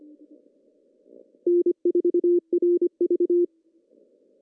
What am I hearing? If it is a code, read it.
Morse code, heard as a keyed tone, about 25 wpm, 380 Hz. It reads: N4RV
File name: CW / cqww-cw-k5tr